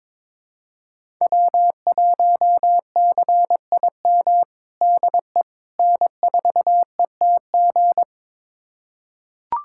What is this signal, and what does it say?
Morse code, heard as a keyed tone, about 22 wpm, 700 Hz. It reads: W1CIM DE N4ETG
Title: USA Callsign DE Callsign - 5 Characters - 22wpm